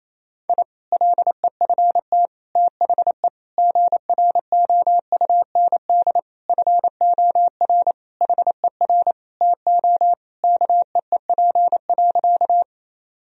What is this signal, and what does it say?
Morse code, heard as a keyed tone, about 28 wpm, 700 Hz. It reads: I LEFT THE GROUND FOR HER TO KEEP.
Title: Sentences from Top 1000 Words - Review Part 2 - 28wpm